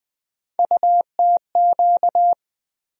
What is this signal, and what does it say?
Morse code, heard as a keyed tone, about 20 wpm, 700 Hz. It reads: UTQ
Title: USA Call Sign Postfixes - 3 Letters 20wpm